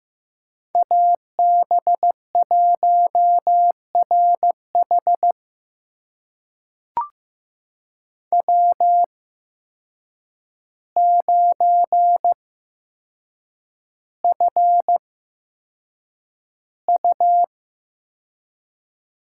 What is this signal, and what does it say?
Morse code, heard as a keyed tone, about 15 wpm, spaced as for 7 wpm, 700 Hz. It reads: AB1RH W9FU
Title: USA Call Signs - 5 Characters - 8x Character Spacing - 15wpm